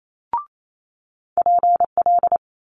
Morse code, heard as a keyed tone, about 28 wpm, 700 Hz. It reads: PL
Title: Multi-letter Phonemes Quick Fire 28wpm